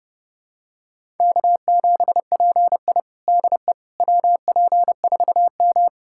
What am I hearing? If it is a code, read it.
Morse code, heard as a keyed tone, about 30 wpm, 700 Hz. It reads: K7PI DE WP4M
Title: USA Callsign DE Callsign - 4 Characters - 30wpm